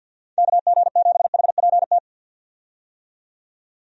Morse code, heard as a keyed tone, about 53 wpm, 700 Hz. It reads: KG7HPT
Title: USA Call Signs - 6 Characters - 50wpm